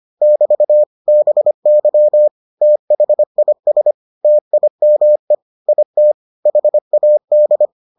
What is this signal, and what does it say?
Morse code, heard as a keyed tone, about 25 wpm, 600 Hz. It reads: <BT> BY THIS TIME IT HAD